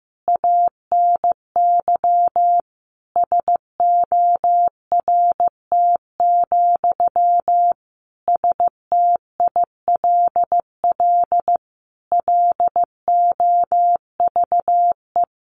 Morse code, heard as a keyed tone, about 15 wpm, 700 Hz. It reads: ANY SORT, STILL LOVE
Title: Sentences from Top 1500 Words 15wpm